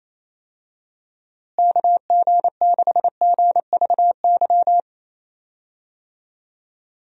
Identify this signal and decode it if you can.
Morse code, heard as a keyed tone, about 28 wpm, 700 Hz. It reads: KG6GVY
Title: USA Call Signs - 6 Characters - Repeated 3x - 28wpm